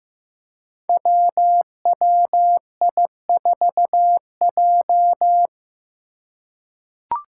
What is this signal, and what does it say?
Morse code, heard as a keyed tone, about 15 wpm, 700 Hz. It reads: WWI4J
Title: Five Challenging Letter-Number Combinations 15wpm